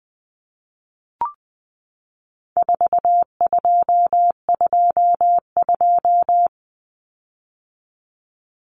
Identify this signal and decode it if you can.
Morse code, heard as a keyed tone, about 20 wpm, 700 Hz. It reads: 4222
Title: Common Cut Numbers - 4-Digit 20wpm